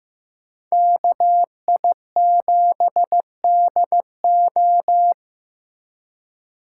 Morse code, heard as a keyed tone, about 18 wpm, 700 Hz. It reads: KI7DO
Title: Speed Racing - USA 2x2 Call Signs 15wpm